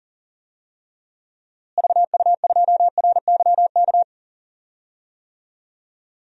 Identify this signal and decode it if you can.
Morse code, heard as a keyed tone, about 40 wpm, 700 Hz. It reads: VU2RYK
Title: DXCC-100 Call Signs - 6 Characters - Repeated 3x - 40wpm